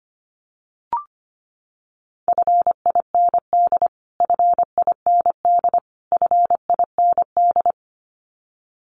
Morse code, heard as a keyed tone, about 25 wpm, 700 Hz. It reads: FIND FIND FIND
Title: Top 300 Words Repeated 25wpm